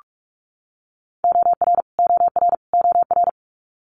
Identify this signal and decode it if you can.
Morse code, heard as a keyed tone, about 46 wpm, 700 Hz. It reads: OR OR OR
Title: Multi-letter Phonemes Repeated 3x 45wpm